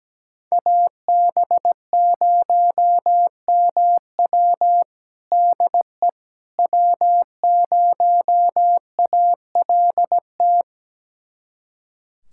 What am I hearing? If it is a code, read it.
Morse code, heard as a keyed tone, about 17 wpm, 700 Hz. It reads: AB0MW DE W0ALT
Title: USA Callsign DE Callsign - 5 Characters - 17wpm